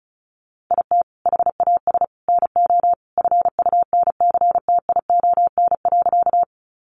Morse code, heard as a keyed tone, about 34 wpm, 700 Hz. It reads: IT HAS NO FUNCTION.
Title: Sentences from Top 1100 Words 35wpm